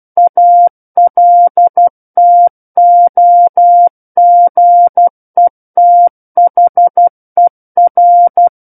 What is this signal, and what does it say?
Morse code, heard as a keyed tone, about 12 wpm, 700 Hz. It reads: ALTOGETHER
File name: Book One / 03-Chapter 3 - 12 WPM